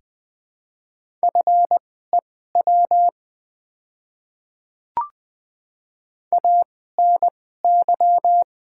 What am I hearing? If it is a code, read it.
Morse code, heard as a keyed tone, about 20 wpm, spaced as for 14 wpm, 700 Hz. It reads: FEW ANY
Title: 2x Character Spacing - Top 300 Words - 20wpm